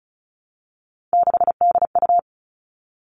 Morse code, heard as a keyed tone, about 36 wpm, 700 Hz. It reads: 6DU
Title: Three Challenging Letter-Number Combinations 35wpm